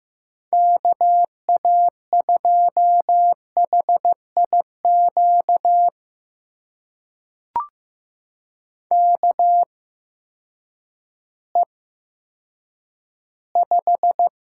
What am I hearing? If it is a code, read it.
Morse code, heard as a keyed tone, about 15 wpm, spaced as for 9 wpm, 700 Hz. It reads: KA2HIQ KE5
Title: USA Call Signs - 6 Characters - 8x Character Spacing - 15wpm